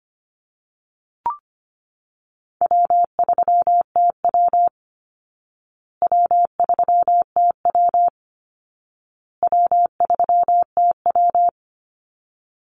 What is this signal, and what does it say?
Morse code, heard as a keyed tone, about 25 wpm, 700 Hz. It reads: W3TW W3TW W3TW
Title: Easy Call Signs Repeated 3x - 25wpm